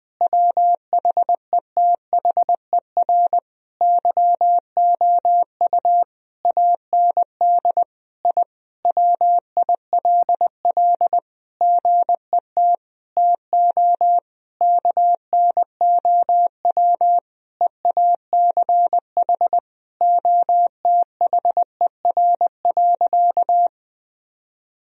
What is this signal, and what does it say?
Morse code, heard as a keyed tone, about 21 wpm, 700 Hz. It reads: WHETHER YOU AND I WILL GET TO KNOW EACH OTHER.
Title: Sentences from Top 1000 Words - Review Part 3 - 20wpm